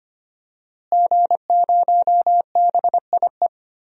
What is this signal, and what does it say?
Morse code, heard as a keyed tone, about 25 wpm, 700 Hz. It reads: G0BIE
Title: DXCC-100 Call Signs - 5 Characters - Repeated 3x - 25wpm